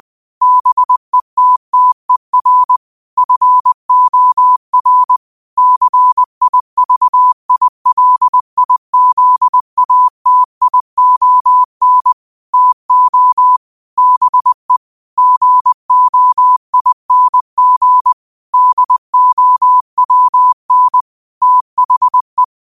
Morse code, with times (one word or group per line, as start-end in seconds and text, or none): 0.4-2.8 BETTER
3.2-5.2 FOR
5.6-12.1 CIVILIZATION
12.5-13.6 TO
14.0-14.8 BE
15.2-18.1 GOING
18.5-21.0 DOWN
21.4-22.4 THE